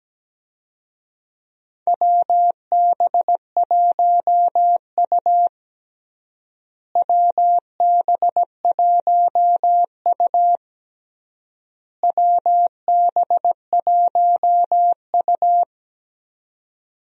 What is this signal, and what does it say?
Morse code, heard as a keyed tone, about 17 wpm, 700 Hz. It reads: WB1U WB1U WB1U
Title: USA Call Signs - 4 Characters - Repeated 3x - 17wpm